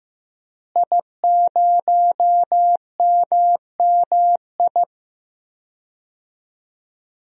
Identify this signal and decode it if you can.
Morse code, heard as a keyed tone, about 15 wpm, 700 Hz. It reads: I0MMI
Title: DXCC-100 Call Signs - 5 Characters - Repeated 3x - 15wpm